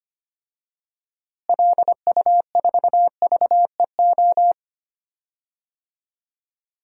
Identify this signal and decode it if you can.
Morse code, heard as a keyed tone, about 25 wpm, 700 Hz. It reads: LU4VEO
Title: DXCC-100 Call Signs - 6 Characters - Repeated 2x - 25wpm